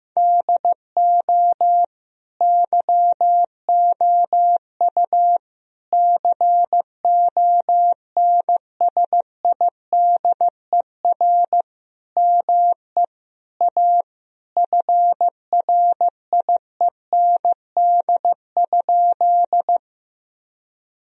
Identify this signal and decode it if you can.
Morse code, heard as a keyed tone, about 15 wpm, 700 Hz. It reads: DO YOU CONSIDER ME A FRIEND?
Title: Sentences from Top 1000 Words - Review Part 4 - 15wpm